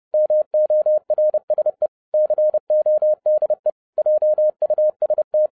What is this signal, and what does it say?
Morse code, heard as a keyed tone, about 30 wpm, 610 Hz. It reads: MORSE CODE JUST
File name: 2012-01-mp3 / n11-I DIDNT MISS THE SOLAR CYCLE 24-30wpm